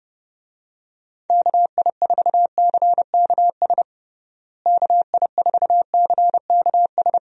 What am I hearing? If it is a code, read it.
Morse code, heard as a keyed tone, about 30 wpm, 700 Hz. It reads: KI4CKS KI4CKS
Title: USA Call Signs - 6 Characters - Repeated 3x - 30wpm